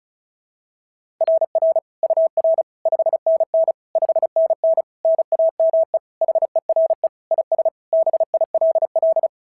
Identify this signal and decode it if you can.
Morse code, heard as a keyed tone, about 36 wpm, 690 Hz. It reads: RR UR 5NN 5NN NAME HERE IS BILL
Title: SOTA QSO Practice 35wpm